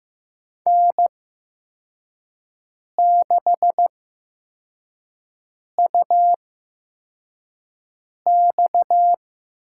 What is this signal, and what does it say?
Morse code, heard as a keyed tone, about 15 wpm, spaced as for 4 wpm, 700 Hz. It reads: N6UX
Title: USA Call Signs - 8x Character Spacing - 15wpm